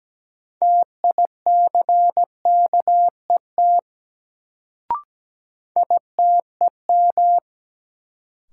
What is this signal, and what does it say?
Morse code, heard as a keyed tone, about 17 wpm, 700 Hz. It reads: TICKET ITEM
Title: Sentences from Top 1400 Words - New Word Drill Alt - 17wpm